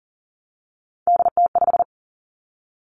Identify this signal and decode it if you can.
Morse code, heard as a keyed tone, about 41 wpm, 700 Hz. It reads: DT5
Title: Three Challenging Letter-Number Combinations 40wpm